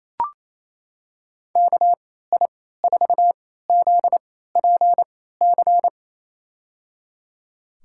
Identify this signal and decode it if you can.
Morse code, heard as a keyed tone, about 28 wpm, spaced as for 16 wpm, 700 Hz. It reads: KI4ZPC
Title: USA Call Signs - 6 Characters - 3x Character Spacing - 28wpm